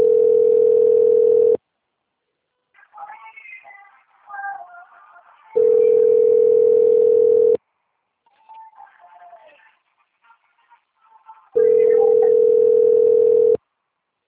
Ringback tone (North America), 440 + 480 Hz.
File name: rg-1600-unknown- - - .40049